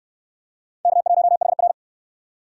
Morse code, heard as a keyed tone, about 66 wpm, 700 Hz. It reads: K1HR